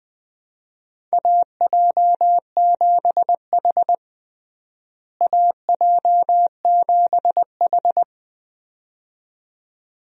Morse code, heard as a keyed tone, about 21 wpm, 700 Hz. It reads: AJ7H AJ7H
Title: USA Call Signs - 4 Characters - Repeated 2x - 20wpm